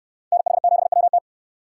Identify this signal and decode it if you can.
Morse code, heard as a keyed tone, about 69 wpm, 700 Hz. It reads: NH7WT